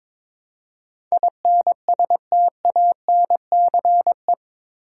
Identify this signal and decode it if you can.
Morse code, heard as a keyed tone, about 22 wpm, 700 Hz. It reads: INSTANCE
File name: practice / 22wpm / Eight Letter Words - 22wpm